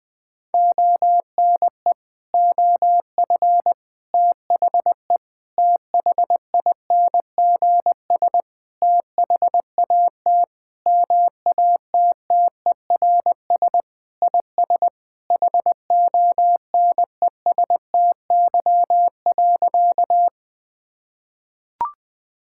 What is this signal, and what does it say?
Morse code, heard as a keyed tone, about 20 wpm, 700 Hz. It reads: ONE OF THE THINGS THAT MATTERS IS HONESTY.
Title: Sentences with 5-Word Phrases - 20wpm